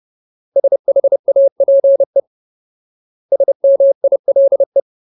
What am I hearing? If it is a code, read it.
Morse code, heard as a keyed tone, about 30 wpm, 550 Hz. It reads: SHAPE SMILE
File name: Ultimate Word List / 30WPM Ultimate